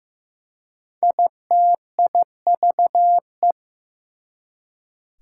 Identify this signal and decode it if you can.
Morse code, heard as a keyed tone, about 15 wpm, 700 Hz. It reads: ITIVE
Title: Prefixes and Suffixes 15wpm